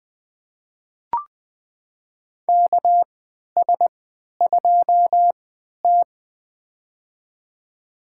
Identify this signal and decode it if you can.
Morse code, heard as a keyed tone, about 21 wpm, spaced as for 11 wpm, 700 Hz. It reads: KS2T
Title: USA Call Signs - 4 Characters - 3x Character Spacing - 20wpm